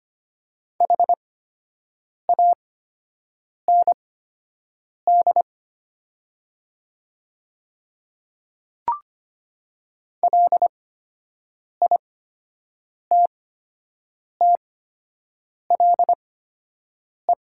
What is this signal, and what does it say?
Morse code, heard as a keyed tone, about 25 wpm, spaced as for 7 wpm, 700 Hz. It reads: HAND LITTLE
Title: 8x Character Spacing - Top 300 Words - 25wpm